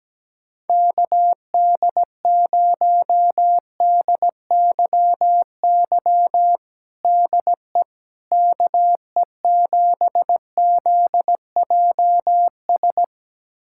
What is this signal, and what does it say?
Morse code, heard as a keyed tone, about 17 wpm, 700 Hz. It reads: KD0DYY DE KE7ZJS
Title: USA Callsign DE Callsign - 6 Characters - 17wpm